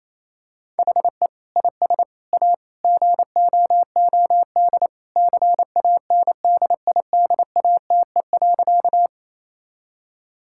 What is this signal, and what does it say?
Morse code, heard as a keyed tone, about 28 wpm, 700 Hz. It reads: HE IS A GOOD CANDIDATE.
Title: Sentences from Top 1600 Words 28wpm